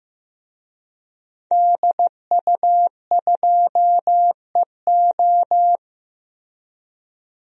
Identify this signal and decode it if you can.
Morse code, heard as a keyed tone, about 15 wpm, 700 Hz. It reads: DU2EO
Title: DXCC-100 Call Signs - 5 Characters - Repeated 3x - 15wpm